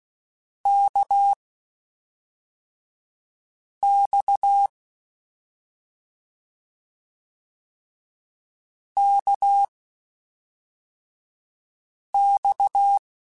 Morse code, heard as a keyed tone, about 16 wpm, spaced as for 3 wpm, 800 Hz. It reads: KX KX